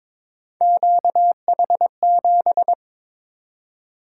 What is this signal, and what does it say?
Morse code, heard as a keyed tone, about 22 wpm, 700 Hz. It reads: QH7
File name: practice / 22wpm / Three Letter-Number Combinations - 22wpm